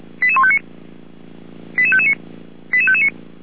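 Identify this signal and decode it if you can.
Five-tone selective call (ZVEI), heard as quick runs of five tones, tones of about 70 ms. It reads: 88159 88559 88559
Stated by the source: ZVEI five-tone selective call 88155 received by SDR.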